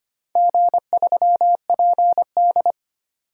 Morse code, heard as a keyed tone, about 25 wpm, 700 Hz. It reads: G3PD
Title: DXCC-100 Call Signs - 4 Characters - 25wpm